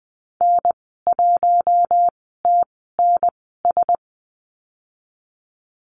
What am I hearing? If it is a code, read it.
Morse code, heard as a keyed tone, about 20 wpm, spaced as for 14 wpm, 700 Hz. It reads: N1TNS
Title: 069 Lesson 12g - Letter d - Call Signs 20wpm